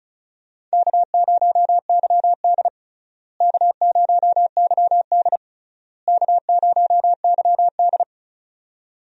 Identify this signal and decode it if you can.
Morse code, heard as a keyed tone, about 36 wpm, 700 Hz. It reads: K0YD K0YD K0YD
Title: USA Call Signs - 4 Characters - Repeated 3x - 35wpm